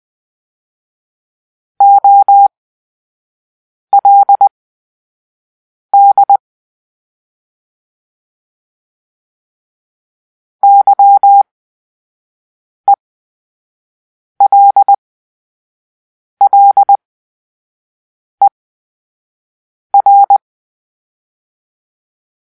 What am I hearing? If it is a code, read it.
Morse code, heard as a keyed tone, about 20 wpm, spaced as for 5 wpm, 800 Hz. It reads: OLD YELLER
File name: Grp-3 Drill-2